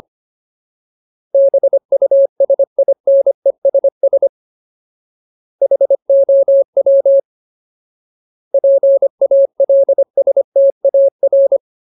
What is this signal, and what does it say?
Morse code, heard as a keyed tone, about 25 wpm, 550 Hz. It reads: BUSINESS HOW PALSTAR
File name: Ultimate Word List / 25WPM Ultimate